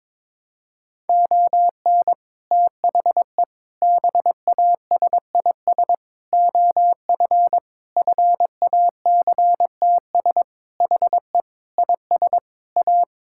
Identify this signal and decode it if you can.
Morse code, heard as a keyed tone, about 22 wpm, 700 Hz. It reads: ON THE BASIS OF FACTS HE IS A
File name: practice / 22wpm / Sentences with 3-Word Phrases - 22wpm